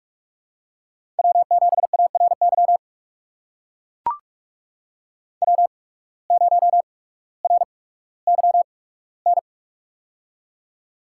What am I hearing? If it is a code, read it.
Morse code, heard as a keyed tone, about 44 wpm, spaced as for 19 wpm, 690 Hz. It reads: W7ARY W0RYN
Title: USA Call Signs - 5 Characters - 8x Character Spacing - 45wpm